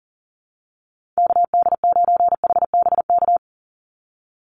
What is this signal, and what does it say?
Morse code, heard as a keyed tone, about 40 wpm, 700 Hz. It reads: KD9HBK
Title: USA Call Signs 40wpm